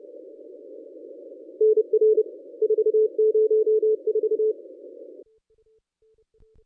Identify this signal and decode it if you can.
Morse code, heard as a keyed tone, about 30 wpm, 430 Hz. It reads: NR 404
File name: contests / ss-cw-k5tr-op-n5rz -1000